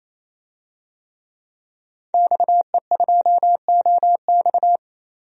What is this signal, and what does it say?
Morse code, heard as a keyed tone, about 28 wpm, 700 Hz. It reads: XE2OX
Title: DXCC-100 Call Signs - 5 Characters - 28wpm